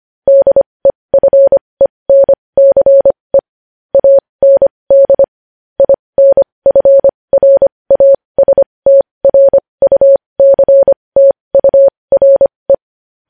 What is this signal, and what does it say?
Morse code, heard as a keyed tone, about 25 wpm, spaced as for 20 wpm, 550 Hz. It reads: DEFENCE AND INFRASTRUCTURE
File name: podcast / News headlines-1x-25x20